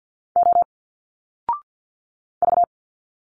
Morse code, heard as a keyed tone, about 50 wpm, 700 Hz. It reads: O V
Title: Single Letters 50wpm